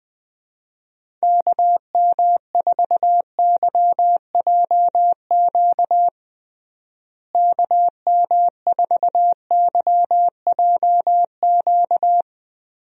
Morse code, heard as a keyed tone, about 20 wpm, 700 Hz. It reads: KM4YJQ KM4YJQ
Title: USA Call Signs - 6 Characters - Repeated 3x - 20wpm